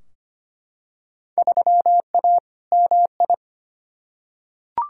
Morse code, heard as a keyed tone, about 25 wpm, 700 Hz. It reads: 3A MI
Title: Field Day Exchanges - 25wpm